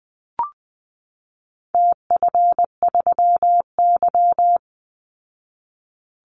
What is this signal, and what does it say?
Morse code, heard as a keyed tone, about 20 wpm, 700 Hz. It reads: TF3Y
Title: Call Signs Repeated 3x - 20wpm